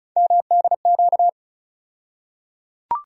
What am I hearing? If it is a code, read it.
Morse code, heard as a keyed tone, about 36 wpm, 700 Hz. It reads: MDQ
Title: USA Call Sign Postfixes - 3 Letters 35wpm